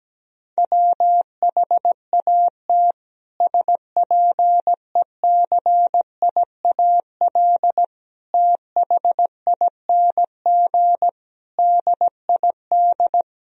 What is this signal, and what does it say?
Morse code, heard as a keyed tone, about 17 wpm, 700 Hz. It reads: WHAT SPECIAL THING DID
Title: Sentences from Top 1000 Words - Review Part 3 - 17wpm